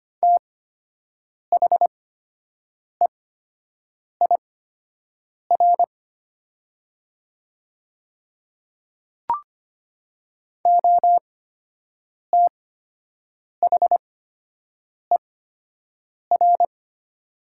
Morse code, heard as a keyed tone, about 25 wpm, spaced as for 7 wpm, 700 Hz. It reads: THEIR OTHER
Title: 8x Character Spacing - Top 100 Words - 25wpm